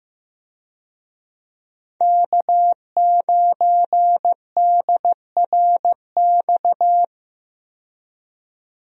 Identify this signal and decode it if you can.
Morse code, heard as a keyed tone, about 15 wpm, 700 Hz. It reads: K9DRX
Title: USA Call Signs - 5 Characters - 15wpm